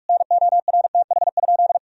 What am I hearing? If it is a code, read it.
Morse code, heard as a keyed tone, about 46 wpm, 700 Hz. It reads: NORTH?
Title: Sentences from Top 1000 Words - Review Part 2 - 45wpm